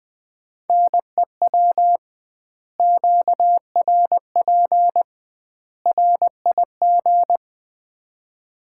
Morse code, heard as a keyed tone, about 20 wpm, 700 Hz. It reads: NEW QRP RIG
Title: Common QSO Phrases - 2x Word-Spacing - 20wpm